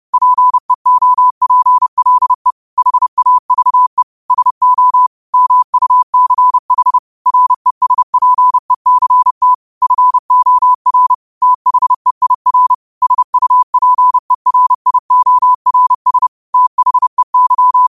Morse code, heard as a keyed tone, about 30 wpm, 1000 Hz. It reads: PEOPLE HAVE SO MUCH RESPECT FOR THEIR SUPERIORS THEY
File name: qotdfiles / 30wpm